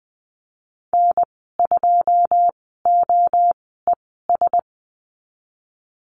Morse code, heard as a keyed tone, about 19 wpm, spaced as for 14 wpm, 700 Hz. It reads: N2OES
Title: 082 Lesson 14g - Number 2 - Call Signs 20wpm